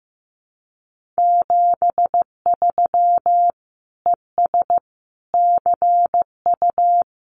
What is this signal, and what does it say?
Morse code, heard as a keyed tone, about 15 wpm, 700 Hz. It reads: 73 ES CU
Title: Common QSO Phrases Encore 15wpm